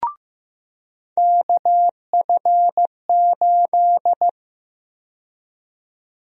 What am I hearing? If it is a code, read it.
Morse code, heard as a keyed tone, about 15 wpm, 700 Hz. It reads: KF8
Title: USA Call Sign Prefixes - 15wpm